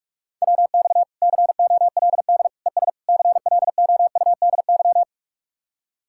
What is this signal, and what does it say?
Morse code, heard as a keyed tone, about 48 wpm, 700 Hz. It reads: WX COLD ES CLOUDY